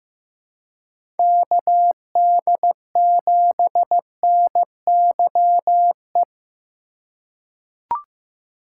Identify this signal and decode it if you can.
Morse code, heard as a keyed tone, about 15 wpm, 700 Hz. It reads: KD7NYE
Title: USA Call Signs - 6 Characters - 4x Character Spacing - 15wpm